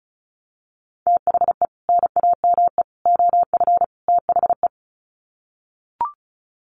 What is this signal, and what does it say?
Morse code, heard as a keyed tone, about 34 wpm, 700 Hz. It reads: THE NAME OF THE
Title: Sets of 4 Words 35wpm